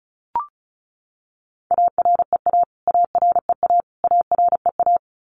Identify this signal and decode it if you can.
Morse code, heard as a keyed tone, about 34 wpm, 700 Hz. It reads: AREA AREA AREA
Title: Top 500 Words Repeated 35wpm